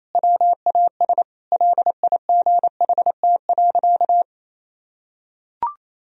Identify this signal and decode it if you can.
Morse code, heard as a keyed tone, about 28 wpm, 700 Hz. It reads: WAS LIGHT.
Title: Sentences from Top 1000 Words - Review Part 1 - 28wpm